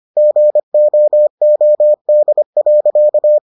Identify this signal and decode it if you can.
Morse code, heard as a keyed tone, about 25 wpm, 600 Hz. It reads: GOOD.